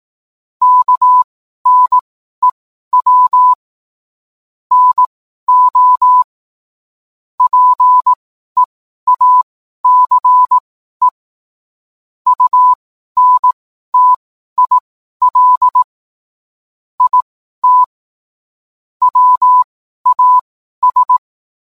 Morse code, heard as a keyed tone, about 18 wpm, spaced as for 12 wpm, 1000 Hz. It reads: KNEW NO PEACE UNTIL IT WAS